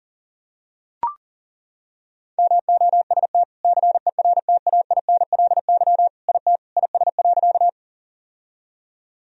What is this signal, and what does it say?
Morse code, heard as a keyed tone, about 40 wpm, 700 Hz. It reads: MOST CERTAINLY IT IS.
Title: Sentences from Top 1000 Words - Review Part 1 - 40wpm